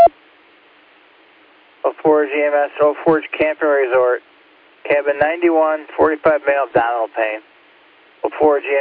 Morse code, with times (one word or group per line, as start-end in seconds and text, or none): none